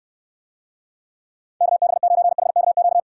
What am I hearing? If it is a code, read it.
Morse code, heard as a keyed tone, about 69 wpm, 700 Hz. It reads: KB9HKQ